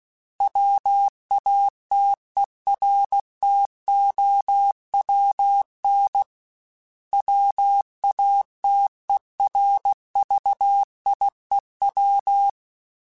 Morse code, heard as a keyed tone, about 16 wpm, 800 Hz. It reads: WATERTOWN WATERVIEW
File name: fromSMs / FromSMs mp3s / Cities@15